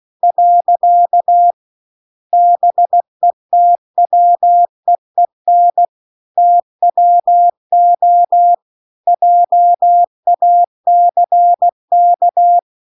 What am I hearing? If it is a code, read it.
Morse code, heard as a keyed tone, about 16 wpm, 700 Hz. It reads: . BETWEEN TWO JACK